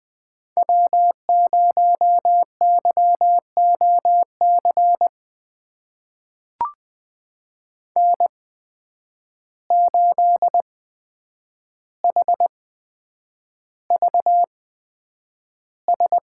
Morse code, heard as a keyed tone, about 20 wpm, spaced as for 9 wpm, 700 Hz. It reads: W0YOC N8HVS
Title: USA Call Signs - 5 Characters - 8x Character Spacing - 20wpm